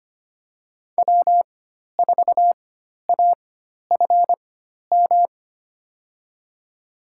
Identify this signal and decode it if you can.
Morse code, heard as a keyed tone, about 25 wpm, spaced as for 12 wpm, 700 Hz. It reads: W4AFM
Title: USA Call Signs - 5 Characters - 4x Character Spacing - 25wpm